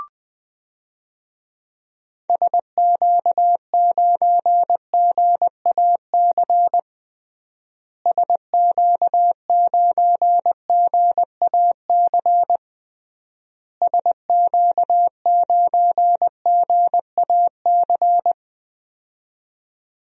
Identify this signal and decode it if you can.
Morse code, heard as a keyed tone, about 21 wpm, 700 Hz. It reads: SQ9GAC SQ9GAC SQ9GAC
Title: DXCC-100 Call Signs - 6 Characters - Repeated 3x - 20wpm